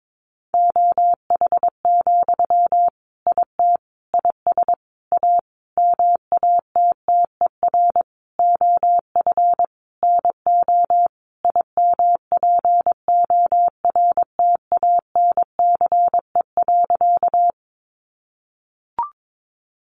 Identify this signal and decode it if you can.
Morse code, heard as a keyed tone, about 22 wpm, 700 Hz. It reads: OH, IT IS A MATTER OF NO IMPORTANCE.
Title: Sentences from Top 1200 Words 22wpm